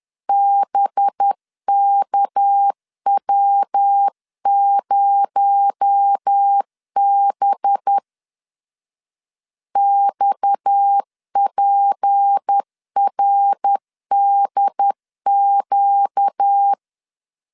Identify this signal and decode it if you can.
Morse code, heard as a keyed tone, about 10 wpm, 800 Hz. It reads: BKW0B XPRDQ
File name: LEZIONI / 6 PKX0 / 6 PKX0 10wpm 2707